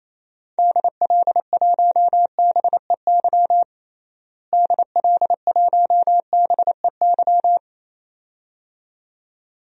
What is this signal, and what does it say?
Morse code, heard as a keyed tone, about 28 wpm, 700 Hz. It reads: DL1BEY DL1BEY
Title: DXCC-100 Call Signs - 6 Characters - Repeated 2x - 28wpm